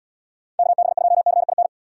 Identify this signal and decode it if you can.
Morse code, heard as a keyed tone, about 71 wpm, 700 Hz. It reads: KB2CIN